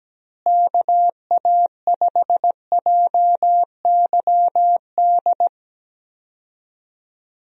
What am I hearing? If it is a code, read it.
Morse code, heard as a keyed tone, about 17 wpm, 700 Hz. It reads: KA5JYD
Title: USA Call Signs - 6 Characters - Repeated 2x - 17wpm